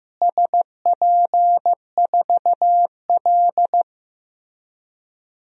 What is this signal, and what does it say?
Morse code, heard as a keyed tone, about 15 wpm, 700 Hz. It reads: SP4L
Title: DXCC-100 Call Signs - 4 Characters - Repeated 3x - 15wpm